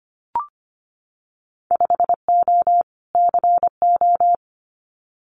Morse code, heard as a keyed tone, about 25 wpm, 690 Hz. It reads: 5O CO
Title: Winter Field Day Exchanges Encore 25wpm